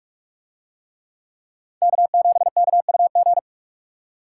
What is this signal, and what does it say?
Morse code, heard as a keyed tone, about 44 wpm, 700 Hz. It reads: K7KUG